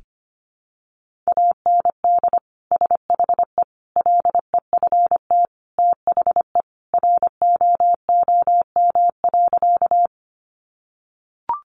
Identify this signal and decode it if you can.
Morse code, heard as a keyed tone, about 25 wpm, 700 Hz. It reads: AND SHE LEFT THE ROOM.
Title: Sentences from Top 300 Words 25wpm